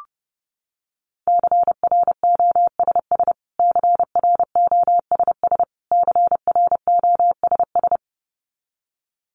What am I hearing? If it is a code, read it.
Morse code, heard as a keyed tone, about 30 wpm, 700 Hz. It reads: CROSS CROSS CROSS
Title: Top 500 Words Repeated 30wpm